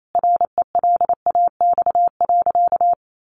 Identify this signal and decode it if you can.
Morse code, heard as a keyed tone, about 28 wpm, 700 Hz. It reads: RELAX.
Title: A Sound Of Thunder 28wpm